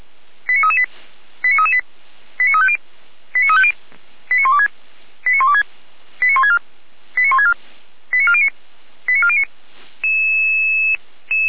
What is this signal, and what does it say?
Five-tone selective call (ZVEI), heard as quick runs of five tones, tones of about 70 ms. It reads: 89338 89338 89350 89350 89126 89126 89164 89164 89409 89409